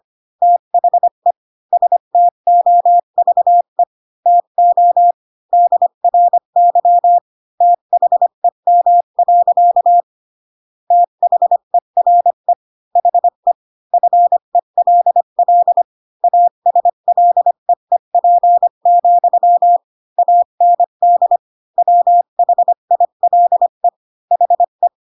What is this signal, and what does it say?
Morse code, heard as a keyed tone, about 25 wpm, 700 Hz. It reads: THE STOVE TO DRY THEM. THERE HE FELL ASLEEP, AND WHILE HE